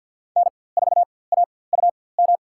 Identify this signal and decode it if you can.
Morse code, heard as a keyed tone, about 49 wpm, spaced as for 23 wpm, 700 Hz. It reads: N4AUM
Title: USA Call Signs - 5 Characters - 4x Character Spacing - 50wpm